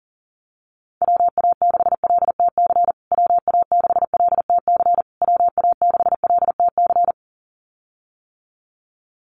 Morse code, heard as a keyed tone, about 40 wpm, 700 Hz. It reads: WA6LTC WA6LTC WA6LTC
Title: USA Call Signs Repeated 3x 40wpm